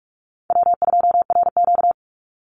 Morse code, heard as a keyed tone, about 44 wpm, 700 Hz. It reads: W2RQ
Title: Easy Call Signs Repeated 2x - 45wpm